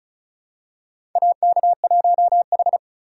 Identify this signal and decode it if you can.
Morse code, heard as a keyed tone, about 36 wpm, 700 Hz. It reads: AK1H
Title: USA Call Signs - 4 Characters - 2x Character Spacing - 35wpm